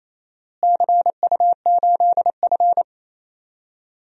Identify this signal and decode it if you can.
Morse code, heard as a keyed tone, about 28 wpm, 700 Hz. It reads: CU8F
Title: DXCC-100 Call Signs - 4 Characters - Repeated 2x - 28wpm